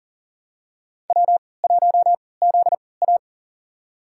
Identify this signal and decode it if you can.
Morse code, heard as a keyed tone, about 40 wpm, spaced as for 23 wpm, 700 Hz. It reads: W1ZA
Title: USA Call Signs - 3x Character Spacing - 40wpm